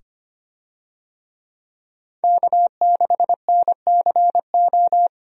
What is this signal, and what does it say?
Morse code, heard as a keyed tone, about 25 wpm, 700 Hz. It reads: K6NCO
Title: USA Call Signs - 5 Characters - 25wpm